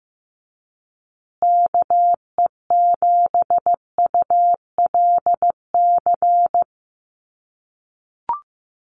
Morse code, heard as a keyed tone, about 15 wpm, 700 Hz. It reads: KE7ULC
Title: USA Call Signs Repeated 2x 15wpm